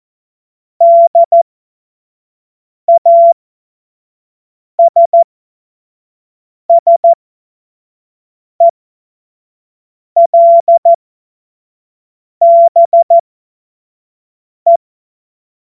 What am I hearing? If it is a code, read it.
Morse code, heard as a keyed tone, about 14 wpm, spaced as for 5 wpm, 680 Hz. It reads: DASSELBE
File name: Morsekurs / Witze / Witz 09. November 2021, 25 F70